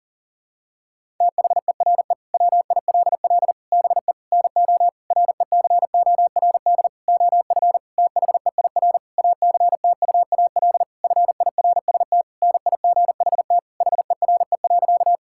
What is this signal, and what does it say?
Morse code, heard as a keyed tone, about 37 wpm, 700 Hz. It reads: THERE WILL BE NO RECORD OF THEIR ACTUAL FIRST NIGHT HERE.
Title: Sentences from Top 1000 Words - Review Part 5 - 40wpm